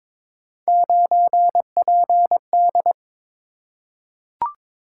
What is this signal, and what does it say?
Morse code, heard as a keyed tone, about 22 wpm, 700 Hz. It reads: 9PD
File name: practice / 22wpm / Three Letter-Number Combinations - 22wpm